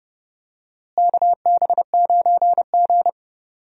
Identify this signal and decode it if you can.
Morse code, heard as a keyed tone, about 30 wpm, 700 Hz. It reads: KB9G
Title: USA Call Signs - 4 Characters - Repeated 2x - 30wpm